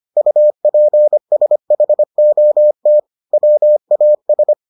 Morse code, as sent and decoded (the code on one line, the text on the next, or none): ..- .--. ... .... --- - / .-- .- ...
UPSHOT WAS